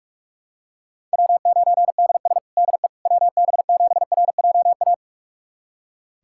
Morse code, heard as a keyed tone, about 42 wpm, 700 Hz. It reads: W9DS DE WB7RJA
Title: USA Callsign DE Callsign 45wpm